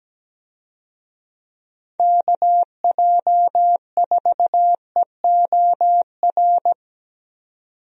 Morse code, heard as a keyed tone, about 17 wpm, 700 Hz. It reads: KJ4EOR
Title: USA Call Signs - 6 Characters - 17wpm